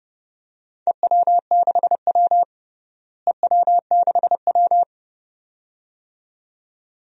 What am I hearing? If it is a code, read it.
Morse code, heard as a keyed tone, about 30 wpm, 700 Hz. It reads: EW6W EW6W
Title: DXCC-100 Call Signs - 4 Characters - Repeated 2x - 30wpm